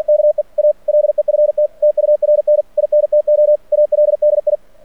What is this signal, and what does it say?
Morse code, heard as a keyed tone, about 49 wpm, 600 Hz. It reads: KE A LEFT TURN INTO ALDI
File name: archived / CRSnet